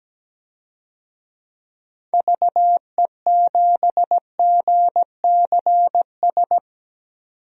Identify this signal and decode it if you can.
Morse code, heard as a keyed tone, about 17 wpm, 700 Hz. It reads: VE7GCS
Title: DXCC-100 Call Signs - 6 Characters - 17wpm